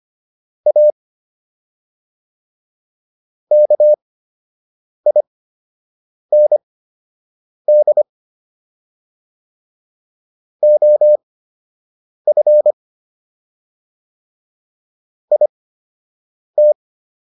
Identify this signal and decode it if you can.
Morse code, heard as a keyed tone, about 25 wpm, spaced as for 7 wpm, 600 Hz. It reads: A KIND OF IT